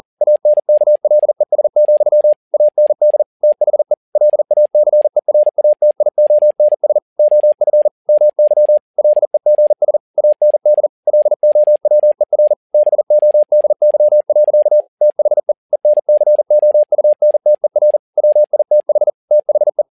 Morse code, heard as a keyed tone, about 40 wpm, 600 Hz. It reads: ANKLES, AND THE LACERATIONS OF MY LEGS AND LOWER BODY. THE ENCOUNTER WITH THE
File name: learning center / tower trouble / tower touble 40wpm